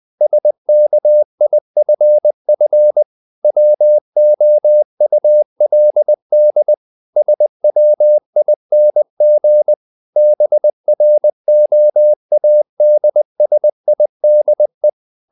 Morse code, heard as a keyed tone, about 20 wpm, 600 Hz. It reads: SKIFF WOULD SWING BROADSIDE
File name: the old man and the sea part 2 20wpm